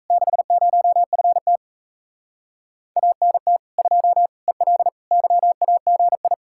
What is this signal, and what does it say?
Morse code, heard as a keyed tone, about 37 wpm, 700 Hz. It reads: 60FT ANT 2 EL YAGI